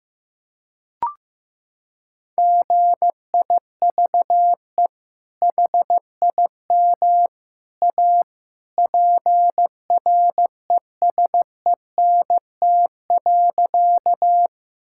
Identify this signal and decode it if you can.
Morse code, heard as a keyed tone, about 15 wpm, 700 Hz. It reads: GIVE HIM A PRESENT.
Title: Sentences from Top 1000 Words - Review Part 5 - 15wpm